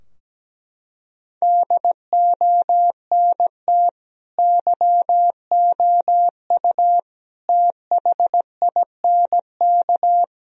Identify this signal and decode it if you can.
Morse code, heard as a keyed tone, about 17 wpm, 700 Hz. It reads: DONT YOU THINK
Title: Sentences from Top 1000 Words - Review Part 4 - 17wpm